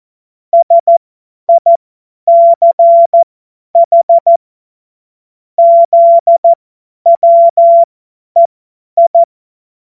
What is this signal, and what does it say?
Morse code, heard as a keyed tone, about 14 wpm, spaced as for 10 wpm, 680 Hz. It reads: SICH ZWEI
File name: Witz-09-September-2025-50-F70